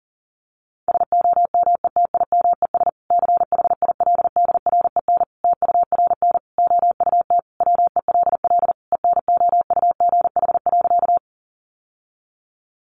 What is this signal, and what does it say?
Morse code, heard as a keyed tone, about 40 wpm, 700 Hz. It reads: SOMETIMES CHILDREN TURN OUT WELL ENOUGH.
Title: Sentences from Top 600 Words 40wpm